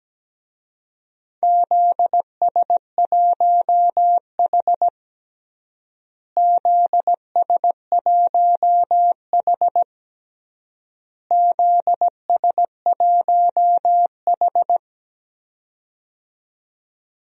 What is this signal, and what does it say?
Morse code, heard as a keyed tone, about 17 wpm, 700 Hz. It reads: ZS1H ZS1H ZS1H
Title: DXCC-100 Call Signs - 4 Characters - Repeated 3x - 17wpm